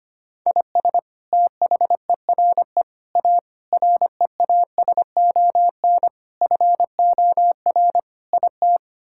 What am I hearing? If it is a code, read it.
Morse code, heard as a keyed tone, about 25 wpm, 700 Hz. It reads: IS THERE A REASON FOR IT
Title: Sentences from Top 1000 Words - Review Part 5 - 25wpm